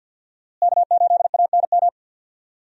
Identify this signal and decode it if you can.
Morse code, heard as a keyed tone, about 47 wpm, 690 Hz. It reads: K8ANM
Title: USA Call Signs - 5 Characters - Repeated 3x - 50wpm